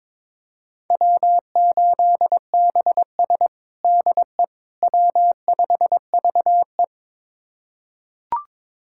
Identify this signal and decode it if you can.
Morse code, heard as a keyed tone, about 22 wpm, 700 Hz. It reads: W8BS DE W5VE
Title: USA Callsign DE Callsign - 4 Characters - 22wpm